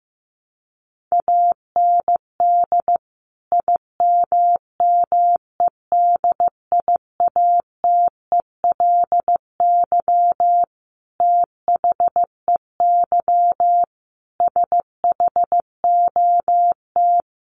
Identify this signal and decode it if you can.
Morse code, heard as a keyed tone, about 15 wpm, 700 Hz. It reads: AND IMMEDIATELY THEY SHOT
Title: Sentences from Top 1100 Words 15wpm